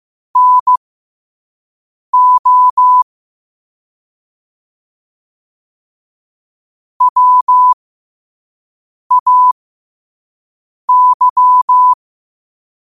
Morse code, heard as a keyed tone, about 15 wpm, spaced as for 5 wpm, 1000 Hz. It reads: NO WAY
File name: qotdfiles / 5wpm